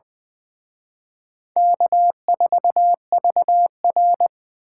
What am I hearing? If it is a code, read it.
Morse code, heard as a keyed tone, about 24 wpm, 700 Hz. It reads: K4VR
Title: Speed Racing - USA Call Signs 20wpm